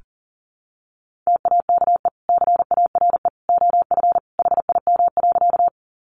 Morse code, heard as a keyed tone, about 40 wpm, 700 Hz. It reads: TAKE CARE OF HIM.
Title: Sentences from Top 300 Words 40wpm